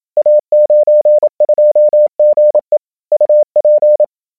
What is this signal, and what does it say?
Morse code, heard as a keyed tone, about 27 wpm, 590 Hz. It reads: A92GE UP
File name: a92ge / SDRQuickAudio